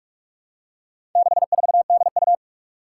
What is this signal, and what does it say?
Morse code, heard as a keyed tone, about 44 wpm, 700 Hz. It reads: 64DU
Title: Four Challenging Letter-Number Combinations 45wpm